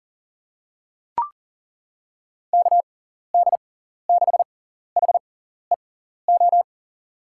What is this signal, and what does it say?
Morse code, heard as a keyed tone, about 40 wpm, spaced as for 14 wpm, 700 Hz. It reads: KD6HEO
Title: USA Call Signs - 6 Characters - 6x Character Spacing - 40wpm